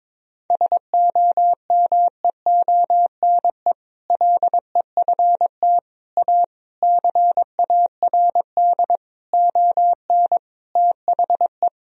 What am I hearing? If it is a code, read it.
Morse code, heard as a keyed tone, about 22 wpm, 700 Hz. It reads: SOMEONE LEFT A CARD ON THE
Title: Sentences from Top 1000 Words - Review Part 4 - 22wpm